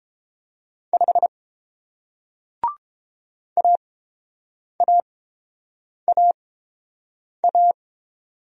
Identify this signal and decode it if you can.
Morse code, heard as a keyed tone, about 30 wpm, spaced as for 7 wpm, 700 Hz. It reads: 5 AAAA